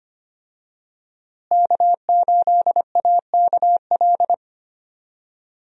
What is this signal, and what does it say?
Morse code, heard as a keyed tone, about 25 wpm, 700 Hz. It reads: K8AKL
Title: USA Call Signs - 5 Characters - Repeated 2x - 25wpm